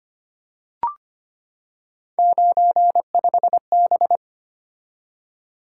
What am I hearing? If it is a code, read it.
Morse code, heard as a keyed tone, about 25 wpm, 700 Hz. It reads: 95B
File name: Three Letter-Number Combinations - 25wpm